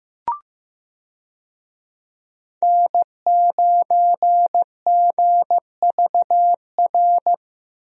Morse code, heard as a keyed tone, about 15 wpm, 700 Hz. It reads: N9GVR
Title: USA Call Signs - 5 Characters - Repeated 2x - 15wpm